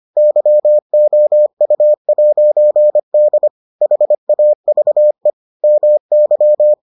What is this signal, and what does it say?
Morse code, heard as a keyed tone, about 25 wpm, 600 Hz. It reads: YOU'D HAVE MY